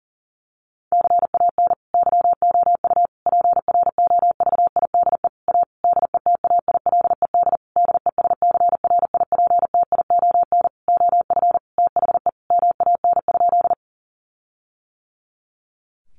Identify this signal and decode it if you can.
Morse code, heard as a keyed tone, about 37 wpm, 700 Hz. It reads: CAN YOU PROVIDE A DETAILED DESCRIPTION OF THE MAN?
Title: Sentences from Top 1500 Words 40wpm